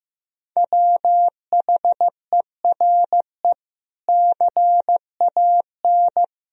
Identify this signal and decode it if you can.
Morse code, heard as a keyed tone, about 15 wpm, 700 Hz. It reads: WHERE CAN
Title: Sentences from Top 500 Words and Abbreviations 15wpm